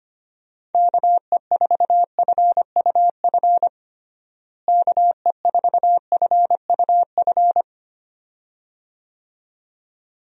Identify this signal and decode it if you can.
Morse code, heard as a keyed tone, about 25 wpm, 700 Hz. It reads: KE4FUF KE4FUF
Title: USA Call Signs - 6 Characters - Repeated 2x - 25wpm